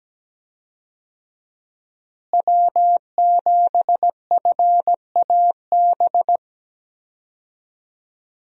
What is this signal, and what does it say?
Morse code, heard as a keyed tone, about 17 wpm, 700 Hz. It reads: W7FAB
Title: USA Call Signs - 5 Characters - Repeated 3x - 17wpm